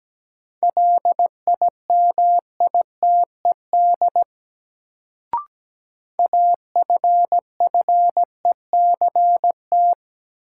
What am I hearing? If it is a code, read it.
Morse code, heard as a keyed tone, about 17 wpm, 700 Hz. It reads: LIMITED AFFECT
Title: Sentences from Top 900 Words - New Word Drill - 17wpm